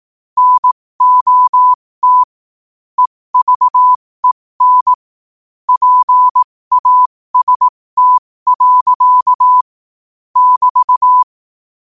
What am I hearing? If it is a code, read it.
Morse code, heard as a keyed tone, about 18 wpm, spaced as for 15 wpm, 1000 Hz. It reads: NOT EVEN PAST. <BT>
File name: qotdfiles / 15wpm